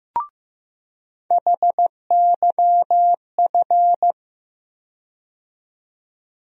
Morse code, heard as a keyed tone, about 15 wpm, 700 Hz. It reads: HYF
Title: USA Call Sign Postfixes - 3 Letters 15wpm